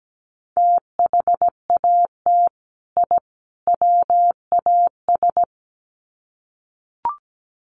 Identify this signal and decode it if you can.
Morse code, heard as a keyed tone, about 17 wpm, 700 Hz. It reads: THAT I WAS
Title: Sets of 3 Words 17wpm